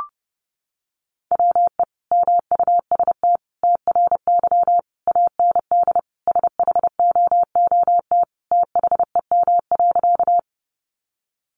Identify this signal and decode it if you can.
Morse code, heard as a keyed tone, about 30 wpm, 700 Hz. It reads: WE MUST TRY AND SHOOT THEM.
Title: Sentences from Top 1200 Words 30wpm